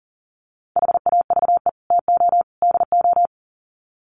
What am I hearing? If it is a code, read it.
Morse code, heard as a keyed tone, about 40 wpm, 700 Hz. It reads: HAVE TO DO
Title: Sets of 3 Words 40wpm